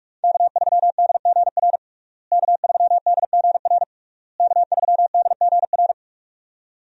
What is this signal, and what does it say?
Morse code, heard as a keyed tone, about 45 wpm, 700 Hz. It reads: K3DGR K3DGR K3DGR
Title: USA Call Signs - 5 Characters - Repeated 3x - 45wpm